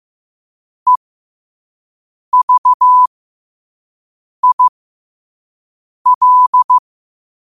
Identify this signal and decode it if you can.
Morse code, heard as a keyed tone, about 15 wpm, spaced as for 5 wpm, 1000 Hz. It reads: EVIL
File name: qotdfiles / 5wpm